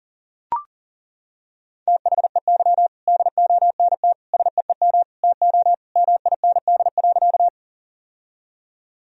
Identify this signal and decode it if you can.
Morse code, heard as a keyed tone, about 40 wpm, 700 Hz. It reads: THEY DONT SEEM TO MIND.
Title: Sentences from Top 1000 Words - Review Part 1 - 40wpm